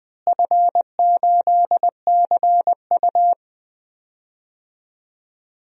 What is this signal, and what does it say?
Morse code, heard as a keyed tone, about 21 wpm, 700 Hz. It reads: F8CU
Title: DXCC-100 Call Signs - 4 Characters - Repeated 2x - 20wpm